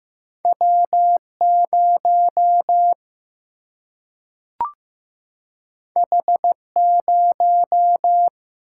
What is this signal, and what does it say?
Morse code, heard as a keyed tone, about 15 wpm, 700 Hz. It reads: W0 H0
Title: Single-Letter Single-Number 15wpm